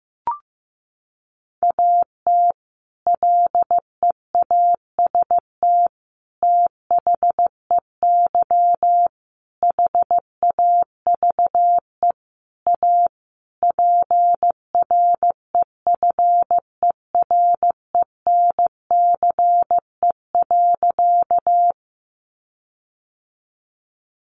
Morse code, heard as a keyed tone, about 15 wpm, 700 Hz. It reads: AT LEAST THEY HAVE A PREFERENCE.
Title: Sentences from Top 1500 Words 15wpm